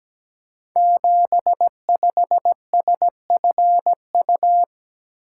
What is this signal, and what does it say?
Morse code, heard as a keyed tone, about 17 wpm, 700 Hz. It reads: 75SFU
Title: Code Groups - 5 Letter:Numbers - 17wpm